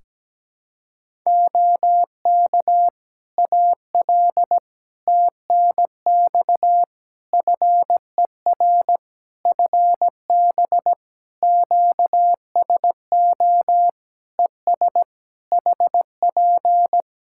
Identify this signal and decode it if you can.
Morse code, heard as a keyed tone, about 17 wpm, 700 Hz. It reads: OK AL TNX FER FB QSO ES HP
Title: Practice QSO - Exchange 4 - Ending - 17wpm